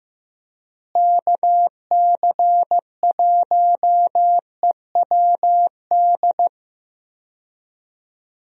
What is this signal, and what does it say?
Morse code, heard as a keyed tone, about 16 wpm, 700 Hz. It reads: KC1EWD
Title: USA Call Signs - 6 Characters - Repeated 2x - 15wpm